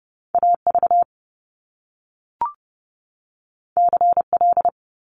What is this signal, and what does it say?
Morse code, heard as a keyed tone, about 30 wpm, 700 Hz. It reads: AV CL
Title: Top Two Letter Combinations in Words 30wpm